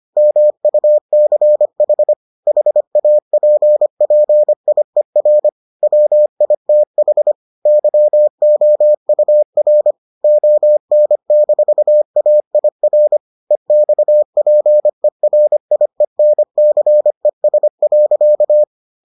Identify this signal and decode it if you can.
Morse code, heard as a keyed tone, about 25 wpm, 600 Hz. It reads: MUCH HAPPIER WITH YOUR ON-AIR EXPERIENCES.